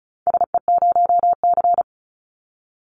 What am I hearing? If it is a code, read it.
Morse code, heard as a keyed tone, about 36 wpm, 700 Hz. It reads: SE0C
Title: Call Signs - 35wpm